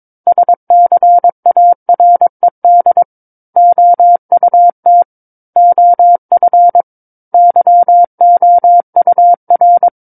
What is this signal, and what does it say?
Morse code, heard as a keyed tone, about 22 wpm, 700 Hz. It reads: SCARED OUT OF YOUR
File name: Book One / 13-Chapter 13 - 22 WPM